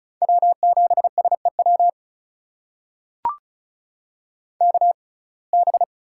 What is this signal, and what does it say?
Morse code, heard as a keyed tone, about 34 wpm, spaced as for 25 wpm, 700 Hz. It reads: W7SEW KB
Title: USA Call Signs - 6x Character Spacing - 35wpm